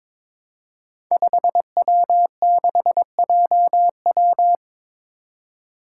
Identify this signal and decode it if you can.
Morse code, heard as a keyed tone, about 22 wpm, 700 Hz. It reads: 5W6JW
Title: Five Challenging Letter-Number Combinations 22wpm